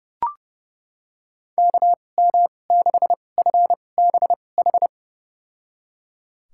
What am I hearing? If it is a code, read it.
Morse code, heard as a keyed tone, about 31 wpm, spaced as for 22 wpm, 700 Hz. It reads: KM6FBH
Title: USA Call Signs - 6 Characters - 2x Character Spacing - 30wpm